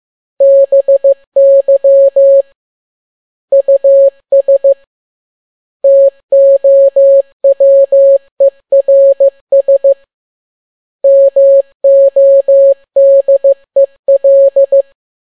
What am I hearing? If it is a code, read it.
Morse code, heard as a keyed tone, about 15 wpm, 550 Hz. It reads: BY US TOWERS MODEL